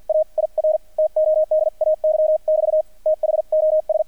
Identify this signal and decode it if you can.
Morse code, heard as a keyed tone, about 48 wpm, spaced as for 41 wpm, 620 Hz, with radio noise. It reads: AIR TODAY<BT> THOS
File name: archived / CRSnet